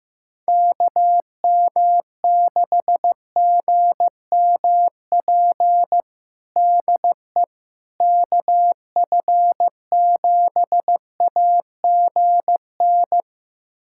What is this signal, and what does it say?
Morse code, heard as a keyed tone, about 15 wpm, 700 Hz. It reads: KM6GMP DE KF7AGN
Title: USA Callsign DE Callsign - 6 Characters - 15wpm